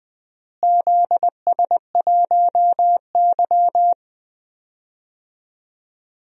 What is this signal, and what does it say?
Morse code, heard as a keyed tone, about 20 wpm, 700 Hz. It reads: ZS1Y
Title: DXCC-100 Call Signs - 4 Characters - Repeated 2x - 20wpm